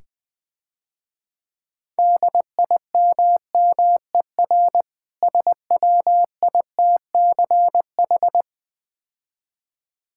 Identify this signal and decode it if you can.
Morse code, heard as a keyed tone, about 20 wpm, 690 Hz. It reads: DIMMER SWITCH
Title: Difficult Words - Speed-Racing with Hints - 20wpm